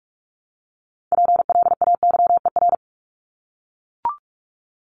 Morse code, heard as a keyed tone, about 44 wpm, 700 Hz. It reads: PLAYER
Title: Top 501-750 Words 45wpm